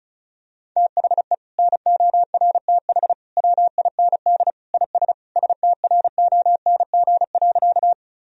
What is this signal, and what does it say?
Morse code, heard as a keyed tone, about 36 wpm, 700 Hz. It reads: THE NORTH WIND IS STRONG.
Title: Sentences from Top 1000 Words - Review Part 2 - 35wpm